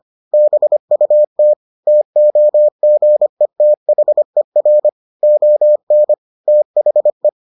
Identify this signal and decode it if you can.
Morse code, heard as a keyed tone, about 25 wpm, 600 Hz. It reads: BUT TOGETHER ON THE